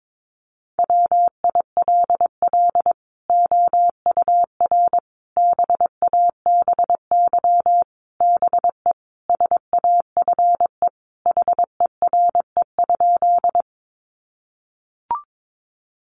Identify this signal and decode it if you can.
Morse code, heard as a keyed tone, about 22 wpm, 700 Hz. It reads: WILL OUR BABY BE SAFE HERE?
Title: Sentences from Top 1300 Words 22wpm